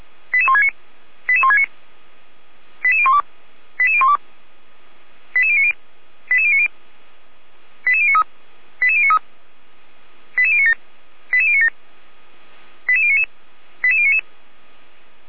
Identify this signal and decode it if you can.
Five-tone selective call (ZVEI), heard as quick runs of five tones, tones of about 70 ms. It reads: 88169 88169 88012 88012 88090 88090 88093 88093 88097 88097 88099 88099